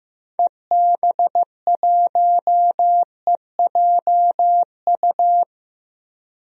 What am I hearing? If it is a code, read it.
Morse code, heard as a keyed tone, about 15 wpm, 700 Hz. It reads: EB1EJU
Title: DXCC-100 Call Signs - 6 Characters - 15wpm